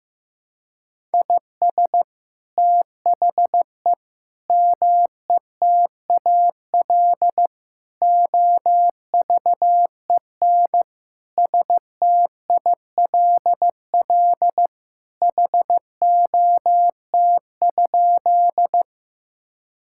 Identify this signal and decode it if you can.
Morse code, heard as a keyed tone, about 15 wpm, 700 Hz. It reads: IS THE METAL OVEN STILL HOT?
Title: Sentences from Top 1000 Words - Review Part 6 - 15wpm